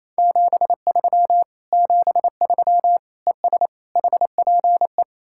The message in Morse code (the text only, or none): 73 73 ES HPE